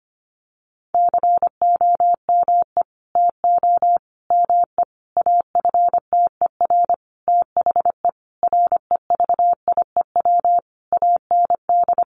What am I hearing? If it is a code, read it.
Morse code, heard as a keyed tone, about 25 wpm, 700 Hz. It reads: COME TO ME AFTER THE REVIEW AND
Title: Sentences from Top 900 Words 25wpm